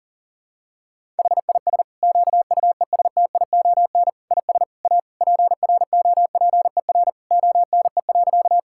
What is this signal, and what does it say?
Morse code, heard as a keyed tone, about 40 wpm, 700 Hz. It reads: HIS QUESTION IS A PROPER ONE.
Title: Sentences from Top 1000 Words - Review Part 5 - 40wpm